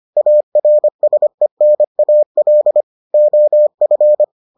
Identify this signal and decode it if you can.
Morse code, heard as a keyed tone, about 25 wpm, 600 Hz. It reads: ARSENAL OF